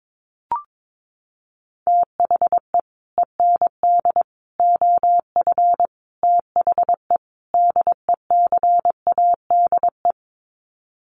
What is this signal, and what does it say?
Morse code, heard as a keyed tone, about 22 wpm, 700 Hz. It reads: THE END OF THE DECADE
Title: Sets of 5 Words Encore 22wpm